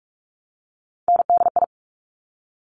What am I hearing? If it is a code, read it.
Morse code, heard as a keyed tone, about 46 wpm, 700 Hz. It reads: NDI
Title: Top Three Letter Combinations in Words 45wpm